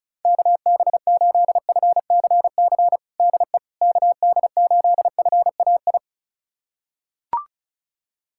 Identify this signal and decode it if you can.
Morse code, heard as a keyed tone, about 36 wpm, 700 Hz. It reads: KB8FCC DE KD8FAI
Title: USA Callsign DE Callsign 35wpm